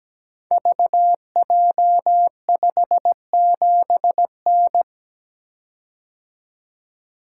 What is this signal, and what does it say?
Morse code, heard as a keyed tone, about 17 wpm, 700 Hz. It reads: VJ57N
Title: Code Groups - 5 Letter:Numbers - 17wpm